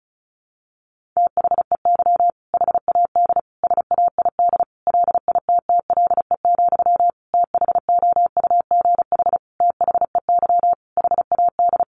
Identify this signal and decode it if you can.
Morse code, heard as a keyed tone, about 34 wpm, 700 Hz. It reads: THEY HAD SAID LITTLE, THOUGH THEY HAD
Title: Sentences from Top 400 Words 35wpm